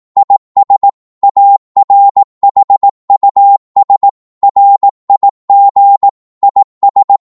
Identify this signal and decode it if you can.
Morse code, heard as a keyed tone, about 18 wpm, 800 Hz. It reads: IS ARHUS RIG IS
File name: Morse / qso-18wpm